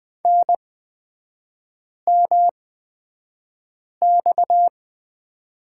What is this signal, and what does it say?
Morse code, heard as a keyed tone, about 20 wpm, spaced as for 5 wpm, 690 Hz. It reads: NMX
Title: 229 Lesson 37e - Letter x - Three Characters 20wpm